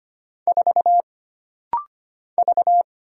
Morse code, heard as a keyed tone, about 25 wpm, 700 Hz. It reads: 4 V
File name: Single Character (Letter:Number:Punctuation:Prosign) - Rapid-Fire - 25wpm